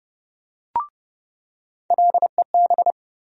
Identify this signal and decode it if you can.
Morse code, heard as a keyed tone, about 30 wpm, 700 Hz. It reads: LEB
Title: USA Call Sign Postfixes - 3 Letters 30wpm